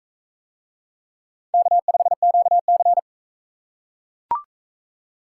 Morse code, heard as a keyed tone, about 40 wpm, 700 Hz. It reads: K5QC